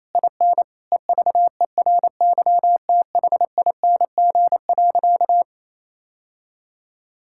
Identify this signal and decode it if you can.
Morse code, heard as a keyed tone, about 28 wpm, 700 Hz. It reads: IN EVERYTHING.
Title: Sentences from Top 1600 Words 28wpm